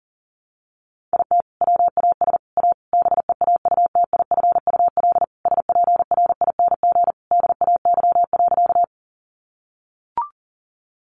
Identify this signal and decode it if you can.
Morse code, heard as a keyed tone, about 40 wpm, 700 Hz. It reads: IT WAS A BEAUTIFUL SPRING DAY.
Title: Sentences from Top 1200 Words 40wpm